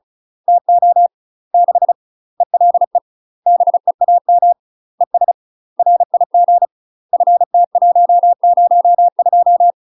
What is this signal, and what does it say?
Morse code, heard as a keyed tone, about 36 wpm, 700 Hz. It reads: TO 6 ELE BEAM ES RIG FT102
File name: UK MASTER QSOs TRAINING / 35WPM / UK MASTER QSOs 0017